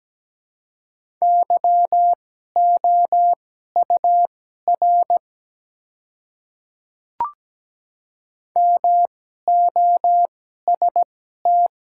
Morse code, heard as a keyed tone, about 17 wpm, spaced as for 12 wpm, 700 Hz. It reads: YOUR MOST
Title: 2x Character Spacing - Top 100 Words - 17wpm